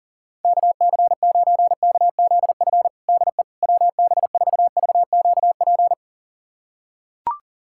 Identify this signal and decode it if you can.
Morse code, heard as a keyed tone, about 37 wpm, 700 Hz. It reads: KC9KZF DE WB4VQP
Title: USA Callsign DE Callsign - 6 Characters - 40wpm